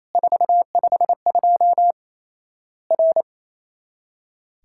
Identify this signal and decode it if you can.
Morse code, heard as a keyed tone, about 28 wpm, 700 Hz. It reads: 452 R
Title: Worked All Europe DX Contest - QTC Exchange - 28wpm